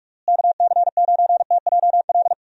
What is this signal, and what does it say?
Morse code, heard as a keyed tone, about 46 wpm, 700 Hz. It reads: KC9TJL
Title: USA Call Signs - 6 Characters - Repeated 3x - 45wpm